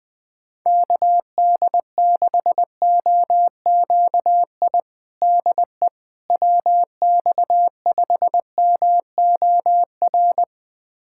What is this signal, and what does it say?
Morse code, heard as a keyed tone, about 20 wpm, 700 Hz. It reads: KD6OQI DE WX5MOR
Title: USA Callsign DE Callsign - 6 Characters - 20wpm